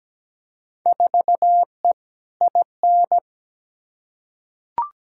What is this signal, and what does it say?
Morse code, heard as a keyed tone, about 17 wpm, 700 Hz. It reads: 4E IN
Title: Field Day Exchanges - 17wpm